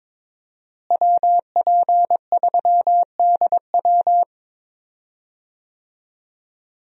Morse code, heard as a keyed tone, about 23 wpm, 700 Hz. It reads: WP3DW
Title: USA Call Signs - 5 Characters - Repeated 2x - 22wpm